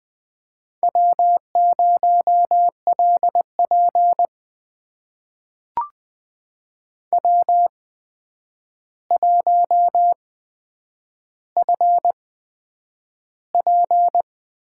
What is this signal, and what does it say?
Morse code, heard as a keyed tone, about 20 wpm, spaced as for 9 wpm, 700 Hz. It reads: W0LP W1FP
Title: USA Call Signs - 4 Characters - 8x Character Spacing - 20wpm